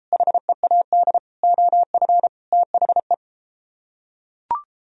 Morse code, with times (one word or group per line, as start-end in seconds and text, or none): 0.1-1.2 HEAD
1.4-2.3 OF
2.5-3.1 THE